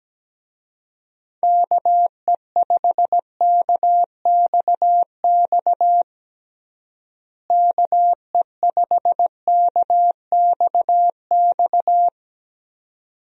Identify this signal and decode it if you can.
Morse code, heard as a keyed tone, about 17 wpm, 700 Hz. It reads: KE5KXX KE5KXX
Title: USA Call Signs - 6 Characters - Repeated 2x - 17wpm